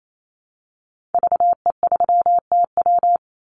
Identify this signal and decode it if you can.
Morse code, heard as a keyed tone, about 28 wpm, 700 Hz. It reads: VE3TW
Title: Call Signs - 28wpm